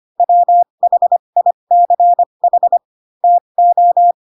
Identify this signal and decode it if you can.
Morse code, heard as a keyed tone, about 25 wpm, spaced as for 22 wpm, 700 Hz. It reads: WHICH TO
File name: Pinocchio / Pinocchio ch6 18 25 ik1hge